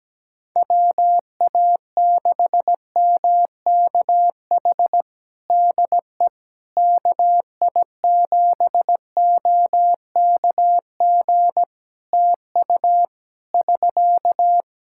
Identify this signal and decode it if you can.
Morse code, heard as a keyed tone, about 17 wpm, 700 Hz. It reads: WA6MKH DE KI7OKG TU <SK>
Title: Practice QSO - Exchange 4 - Ending - 17wpm